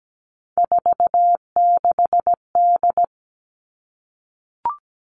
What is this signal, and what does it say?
Morse code, heard as a keyed tone, about 17 wpm, 700 Hz. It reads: 46D
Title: Three Challenging Letter-Number Combinations 17wpm